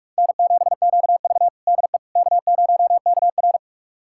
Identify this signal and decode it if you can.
Morse code, heard as a keyed tone, about 45 wpm, 700 Hz. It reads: N7QV DE K0KR
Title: USA Callsign DE Callsign - 4 Characters - 45wpm